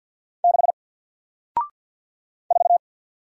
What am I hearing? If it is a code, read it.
Morse code, heard as a keyed tone, about 54 wpm, 700 Hz. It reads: 6 4
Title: Single Letter-Number Alt 50wpm